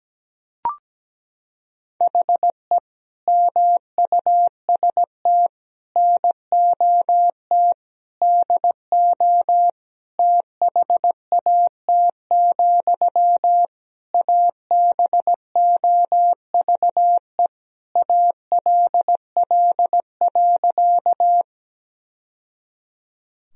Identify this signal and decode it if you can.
Morse code, heard as a keyed tone, about 17 wpm, 700 Hz. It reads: HE MUST NOT DO THAT, ABOVE ALL.
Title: Sentences from Top 1000 Words - Review Part 2 - 17wpm